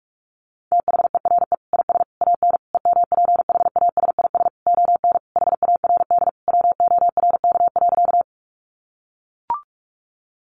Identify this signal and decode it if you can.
Morse code, heard as a keyed tone, about 44 wpm, 700 Hz. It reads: THERE IS AN EMPHASIS ON HARD WORK.
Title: Sentences from Top 1400 Words 45wpm